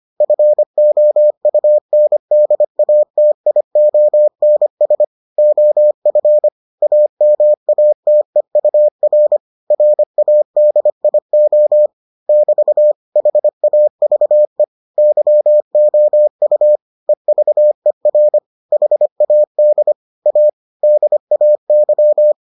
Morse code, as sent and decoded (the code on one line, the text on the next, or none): ..-. --- ..- -. -.. .- - .. --- -. ... / --- ..-. / .- -- .- - . ..- .-. / .-. .- -.. .. --- / -...- / .... .- ...- . / -.-- --- ..- / . ...- . .-. / .... .- -.. / .- / -.. .- -.--
FOUNDATIONS OF AMATEUR RADIO <BT> HAVE YOU EVER HAD A DAY